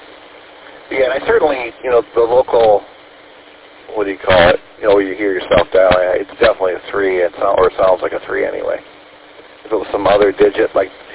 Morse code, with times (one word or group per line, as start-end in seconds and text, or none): none